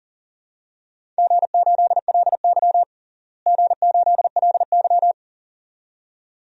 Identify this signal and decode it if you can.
Morse code, heard as a keyed tone, about 40 wpm, 700 Hz. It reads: G8LY G8LY
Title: DXCC-100 Call Signs - 4 Characters - Repeated 2x - 40wpm